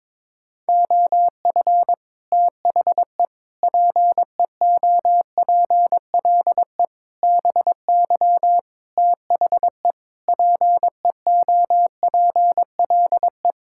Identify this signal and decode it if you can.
Morse code, heard as a keyed tone, about 22 wpm, 700 Hz. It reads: OF THE PEOPLE BY THE PEOPLE
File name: Sets of 9 Words - 22wpm